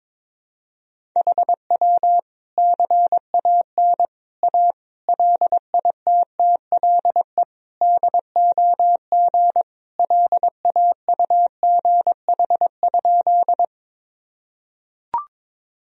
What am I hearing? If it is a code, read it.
Morse code, heard as a keyed tone, about 22 wpm, 700 Hz. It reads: HW CAN A LITTLE DOG LAUGH?
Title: Sentences from Top 500 Words and Abbreviations 22wpm